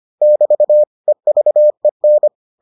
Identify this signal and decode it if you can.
Morse code, heard as a keyed tone, about 25 wpm, 600 Hz. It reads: <BT> EVEN